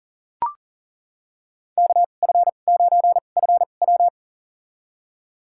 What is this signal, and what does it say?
Morse code, heard as a keyed tone, about 39 wpm, spaced as for 29 wpm, 700 Hz. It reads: KF9FW
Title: USA Call Signs - 2x Character Spacing - 40wpm